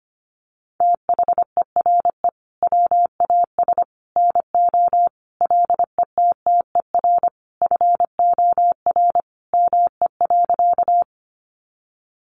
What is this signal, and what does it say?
Morse code, heard as a keyed tone, about 25 wpm, 700 Hz. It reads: THERE WAS NO LETTER FOR ME.
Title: Sentences from Top 300 Words 25wpm